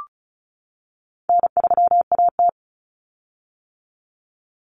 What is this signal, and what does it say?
Morse code, heard as a keyed tone, about 34 wpm, 700 Hz. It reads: N3AT
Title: Common Cut Numbers - 4-Digit 35wpm